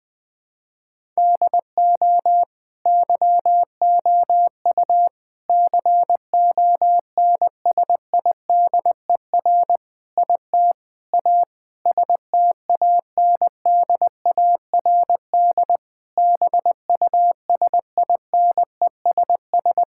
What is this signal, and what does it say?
Morse code, heard as a keyed tone, about 20 wpm, 700 Hz. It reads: DO YOU CONSIDER IT A STANDARD BUSINESS
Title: Sentences from Top 1000 Words - Review Part 6 - 20wpm